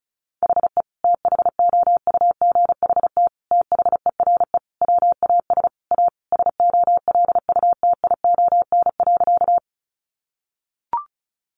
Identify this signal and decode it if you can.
Morse code, heard as a keyed tone, about 34 wpm, 700 Hz. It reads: HE THOUGHT THERE WAS A SOLUTION.
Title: Sentences from Top 1300 Words 35wpm